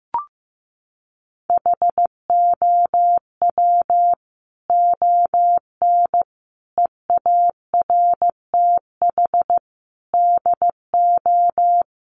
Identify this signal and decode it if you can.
Morse code, heard as a keyed tone, about 15 wpm, 700 Hz. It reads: HOW ON EARTH DO
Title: Sentences from Top 200 Words 15wpm